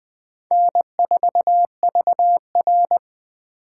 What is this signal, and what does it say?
Morse code, heard as a keyed tone, about 20 wpm, 700 Hz. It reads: N4VR
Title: USA Call Signs - 4 Characters - 4x Character Spacing - 20wpm